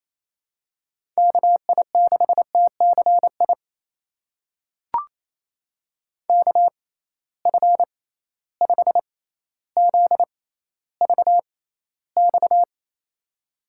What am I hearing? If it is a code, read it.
Morse code, heard as a keyed tone, about 28 wpm, spaced as for 14 wpm, 700 Hz. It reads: KI6TCI KF5ZVX
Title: USA Call Signs - 6 Characters - 6x Character Spacing - 28wpm